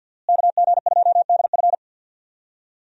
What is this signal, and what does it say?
Morse code, heard as a keyed tone, about 49 wpm, 700 Hz. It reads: KG2DF
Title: USA Call Signs - 5 Characters - Repeated 3x - 50wpm